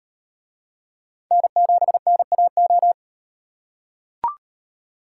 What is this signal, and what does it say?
Morse code, heard as a keyed tone, about 38 wpm, 700 Hz. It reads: N7NAO